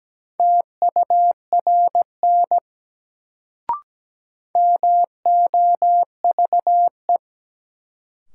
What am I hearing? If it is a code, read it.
Morse code, heard as a keyed tone, about 17 wpm, 700 Hz. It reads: TURN MOVE
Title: Sentences from Top 200 Words - New Word Drill - 17wpm